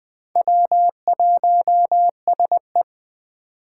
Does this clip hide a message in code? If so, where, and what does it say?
Yes: Morse code from 0.4 to 2.8 s, heard as a keyed tone, about 22 wpm, 700 Hz. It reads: W1SE